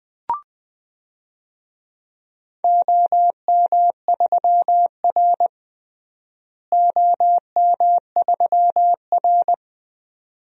Morse code, heard as a keyed tone, about 21 wpm, 700 Hz. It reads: OM3R OM3R
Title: DXCC-100 Call Signs - 4 Characters - Repeated 2x - 20wpm